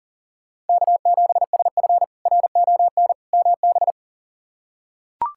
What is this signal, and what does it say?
Morse code, heard as a keyed tone, about 40 wpm, 700 Hz. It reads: K7SF RON MB
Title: K1USN SST QSO Practice - 40wpm